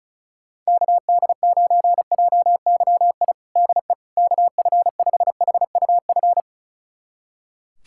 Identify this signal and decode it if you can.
Morse code, heard as a keyed tone, about 36 wpm, 700 Hz. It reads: KD9JYI DE KF5HUF
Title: USA Callsign DE Callsign 35wpm